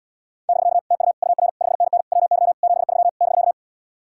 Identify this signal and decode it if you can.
Morse code, heard as a keyed tone, about 77 wpm, 700 Hz. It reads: <BK> TU UR 5NN WY 73 <BK>